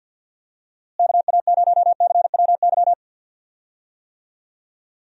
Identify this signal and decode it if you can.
Morse code, heard as a keyed tone, about 46 wpm, 700 Hz. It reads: KA0CWY
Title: USA Call Signs - 6 Characters - Repeated 3x - 50wpm